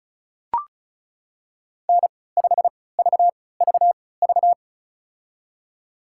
Morse code, heard as a keyed tone, about 34 wpm, spaced as for 20 wpm, 700 Hz. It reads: N5VVV
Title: USA Call Signs - 5 Characters - 3x Character Spacing - 35wpm